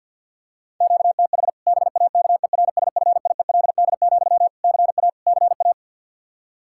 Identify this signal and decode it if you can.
Morse code, heard as a keyed tone, about 51 wpm, 700 Hz. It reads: QTH BAKERSFIELD, CA CA
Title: Practice QSO - Exchange 1 - Beginning - 50wpm